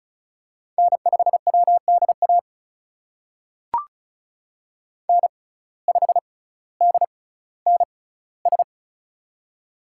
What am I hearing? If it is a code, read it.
Morse code, heard as a keyed tone, about 36 wpm, spaced as for 18 wpm, 700 Hz. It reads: N5WDA N5DNS
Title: USA Call Signs - 5 Characters - 6x Character Spacing - 35wpm